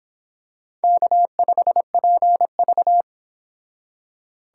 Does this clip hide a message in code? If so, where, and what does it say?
Yes: Morse code from 0.8 to 3.0 s, heard as a keyed tone, about 25 wpm, 700 Hz. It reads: K5PV